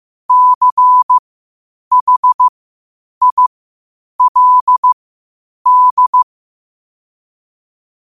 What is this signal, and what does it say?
Morse code, heard as a keyed tone, about 15 wpm, spaced as for 8 wpm, 1000 Hz. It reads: CHILD
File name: qotdfiles / 8wpm